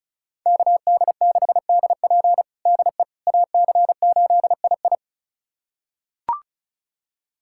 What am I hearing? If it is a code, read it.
Morse code, heard as a keyed tone, about 34 wpm, 700 Hz. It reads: KD6DP DE AC8II
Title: USA Callsign DE Callsign - 5 Characters - 35wpm